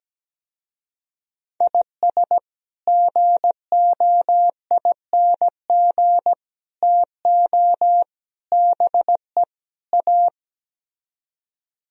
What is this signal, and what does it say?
Morse code, heard as a keyed tone, about 17 wpm, 700 Hz. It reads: IS GOING TO BE A
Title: Speed-Racing - Sets of 5 Words 17wpm